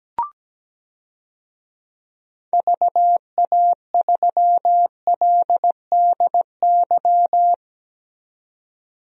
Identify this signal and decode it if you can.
Morse code, heard as a keyed tone, about 17 wpm, 700 Hz. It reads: VA3LDY
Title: DXCC-100 Call Signs - 6 Characters - 17wpm